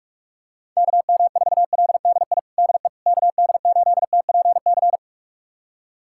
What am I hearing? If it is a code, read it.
Morse code, heard as a keyed tone, about 46 wpm, 700 Hz. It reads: KM4LDI DE KD8TPC
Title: USA Callsign DE Callsign - 6 Characters - 45wpm